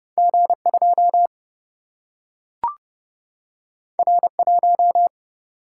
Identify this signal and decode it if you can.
Morse code, heard as a keyed tone, about 30 wpm, 700 Hz. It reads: G2 R1
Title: Single-Letter Single-Number 30wpm